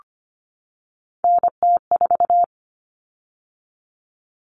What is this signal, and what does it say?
Morse code, heard as a keyed tone, about 25 wpm, 700 Hz. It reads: NT4
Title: Common Cut Numbers - 3-Digit 25wpm